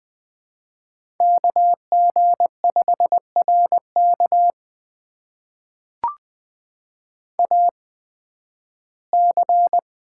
Morse code, heard as a keyed tone, about 20 wpm, spaced as for 13 wpm, 700 Hz. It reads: KG5RK AC
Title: USA Call Signs - 8x Character Spacing - 20wpm